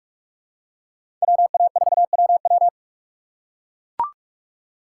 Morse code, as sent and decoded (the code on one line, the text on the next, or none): .-- .- ....- .-- .--
WA4WW